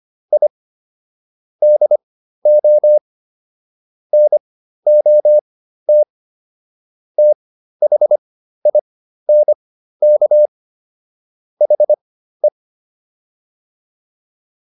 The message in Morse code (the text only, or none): I DO NOT THINK HE